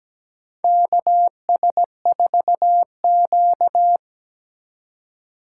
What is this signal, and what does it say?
Morse code, heard as a keyed tone, about 17 wpm, 700 Hz. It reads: KS4Q
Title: USA Call Signs - 4x Character Spacing - 17wpm